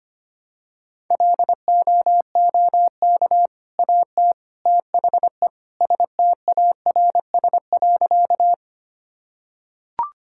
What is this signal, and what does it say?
Morse code, heard as a keyed tone, about 25 wpm, 700 Hz. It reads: LOOK AT THE STARS.
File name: practice / 25wpm / Sentences with 3-Word Phrases - 25wpm